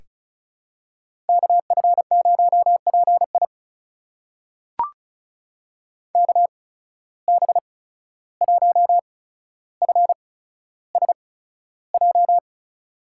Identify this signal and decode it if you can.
Morse code, heard as a keyed tone, about 36 wpm, spaced as for 14 wpm, 700 Hz. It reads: KF0PI KB1FSJ
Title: USA Call Signs - 8x Character Spacing - 35wpm